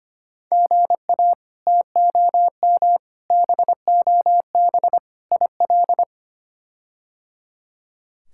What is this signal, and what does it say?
Morse code, heard as a keyed tone, about 25 wpm, 700 Hz. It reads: GA TOM BOB IL
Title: K1USN SST QSO Practice - 25wpm